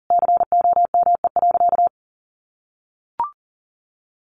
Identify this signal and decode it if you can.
Morse code, heard as a keyed tone, about 40 wpm, 700 Hz. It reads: COME.
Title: Sentences from Top 1500 Words 40wpm